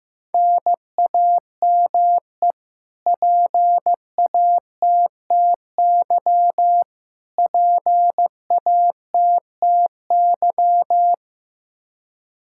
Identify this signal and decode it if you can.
Morse code, heard as a keyed tone, about 15 wpm, 700 Hz. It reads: NAME PATTY PATTY
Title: Practice QSO - Exchange 1 - Beginning - 15wpm